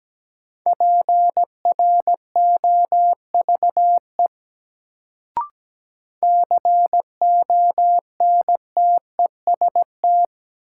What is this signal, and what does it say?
Morse code, heard as a keyed tone, about 17 wpm, 700 Hz. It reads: PROVE CONTEST
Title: Sentences from Top 1600 Words - New Word Drill Alt - 17wpm